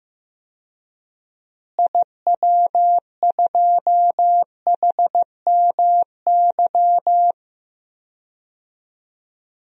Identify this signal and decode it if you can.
Morse code, heard as a keyed tone, about 15 wpm, 700 Hz. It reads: IW2HMY
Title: DXCC-100 Call Signs - 6 Characters - Repeated 2x - 15wpm